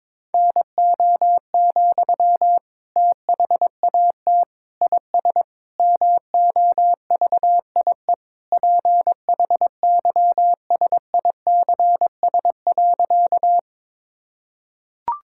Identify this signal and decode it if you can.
Morse code, heard as a keyed tone, about 21 wpm, 700 Hz. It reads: NO, THAT IS MOVIE PHYSICS.
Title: Sentences from Top 1000 Words - Review Part 5 - 22wpm